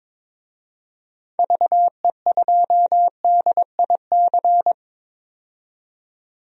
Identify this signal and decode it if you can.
Morse code, heard as a keyed tone, about 22 wpm, 700 Hz. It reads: VE2DIC
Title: DXCC-100 Call Signs - 6 Characters - 22wpm